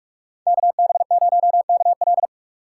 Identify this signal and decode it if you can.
Morse code, heard as a keyed tone, about 45 wpm, 700 Hz. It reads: KB0KL